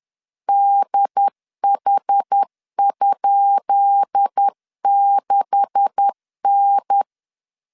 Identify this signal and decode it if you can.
Morse code, heard as a keyed tone, about 10 wpm, 800 Hz. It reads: DH?6N
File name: LEZIONI / 10 SIGNS / 10B INT 10wpm 2844